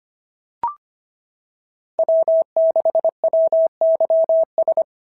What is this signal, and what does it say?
Morse code, heard as a keyed tone, about 24 wpm, 650 Hz. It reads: W6WYS
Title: SOTA QSO Practice 25wpm